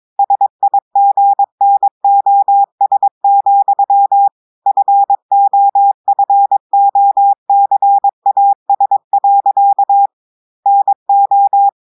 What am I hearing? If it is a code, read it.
Morse code, heard as a keyed tone, about 22 wpm, 800 Hz. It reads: SIGNOS, FOFOCAS. NO
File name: audios / portugais / portugais texte en clair 2 vitesse 22 mots minutes